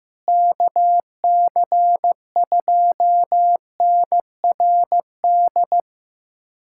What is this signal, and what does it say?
Morse code, heard as a keyed tone, about 15 wpm, 700 Hz. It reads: KC2NRD
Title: USA Callsign DE Callsign 15wpm